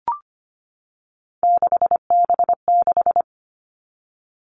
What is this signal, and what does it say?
Morse code, heard as a keyed tone, about 25 wpm, 700 Hz. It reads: 6B6
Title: All Cut Numbers - 3-Digit 25wpm